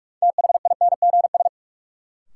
Morse code, heard as a keyed tone, about 46 wpm, 700 Hz. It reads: THINGS
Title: Sentences from Top 1000 Words - Review Part 4 - New Word Drill - 45wpm